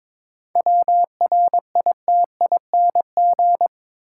Morse code, heard as a keyed tone, about 21 wpm, 700 Hz. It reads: WRITING
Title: Sentences from Top 800 Words - New Word Drill - 22wpm